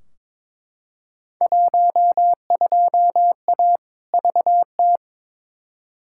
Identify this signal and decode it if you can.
Morse code, heard as a keyed tone, about 22 wpm, 700 Hz. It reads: 12A VT
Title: Field Day Exchanges Encore - 22wpm